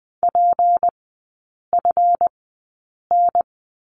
Morse code, heard as a keyed tone, about 20 wpm, spaced as for 8 wpm, 700 Hz. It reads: PFN
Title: 152 Lesson 25e - Letter p - Three Characters 20wpm